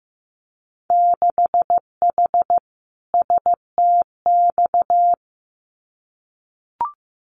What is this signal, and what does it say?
Morse code, heard as a keyed tone, about 15 wpm, 700 Hz. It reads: 6H STX
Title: Winter Field Day Exchanges Encore 15wpm